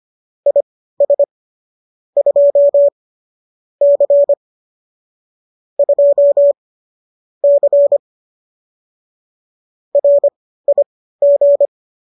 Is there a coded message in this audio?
Yes: Morse code from 0.5 to 11.7 s, heard as a keyed tone, about 25 wpm, spaced as for 15 wpm, 570 Hz. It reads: IS 2 C 2 C RIG